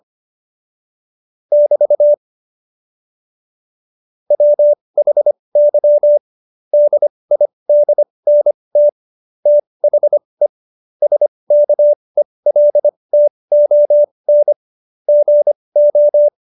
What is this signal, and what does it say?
Morse code, heard as a keyed tone, about 25 wpm, spaced as for 20 wpm, 590 Hz. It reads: <BT> WHY DIDNT THE SKELTON GO
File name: Jokes-25-char-20-eff